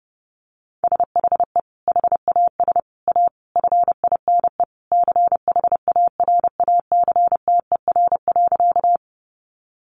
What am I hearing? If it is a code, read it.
Morse code, heard as a keyed tone, about 30 wpm, 700 Hz. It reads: SHE HAS A FINE CHARACTER.
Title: Sentences from Top 1200 Words 30wpm